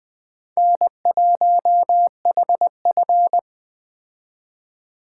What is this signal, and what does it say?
Morse code, heard as a keyed tone, about 22 wpm, 700 Hz. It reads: N1HF